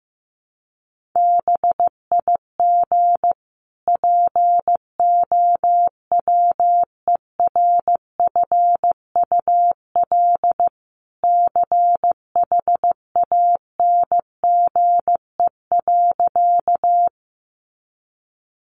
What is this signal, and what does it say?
Morse code, heard as a keyed tone, about 15 wpm, 700 Hz. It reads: BIG POWERFUL CHANGE.
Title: Sentences from Top 1300 Words 15wpm